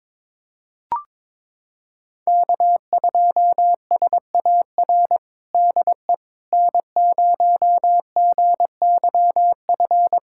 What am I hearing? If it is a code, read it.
Morse code, heard as a keyed tone, about 22 wpm, 700 Hz. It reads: K2SAR DE N0GYF
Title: USA Callsign DE Callsign - 5 Characters - 22wpm